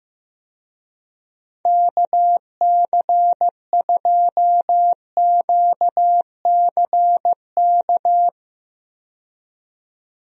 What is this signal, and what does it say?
Morse code, heard as a keyed tone, about 15 wpm, 700 Hz. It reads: KC2QCK
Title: USA Call Signs - 6 Characters - 15wpm